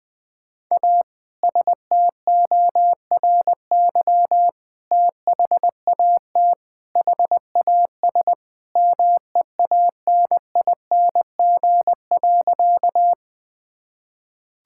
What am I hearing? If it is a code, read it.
Morse code, heard as a keyed tone, about 20 wpm, 700 Hz. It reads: A STORY THAT HAS MEANING.
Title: Sentences with 2-Word Phrases - 20wpm